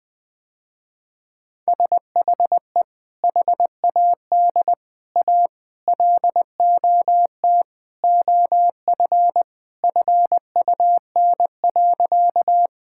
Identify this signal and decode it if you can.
Morse code, heard as a keyed tone, about 21 wpm, 700 Hz. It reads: SHE HAD A LOT OF FUN.
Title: Sentences with 4-Word Phrases - Spoken Beginning - 20wpm